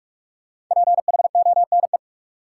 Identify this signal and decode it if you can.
Morse code, heard as a keyed tone, about 43 wpm, 700 Hz. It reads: PHONE
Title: Sentences from Top 1100 Words - New Word Drill - 45wpm